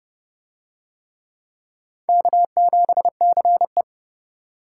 Morse code, heard as a keyed tone, about 27 wpm, 700 Hz. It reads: K7CE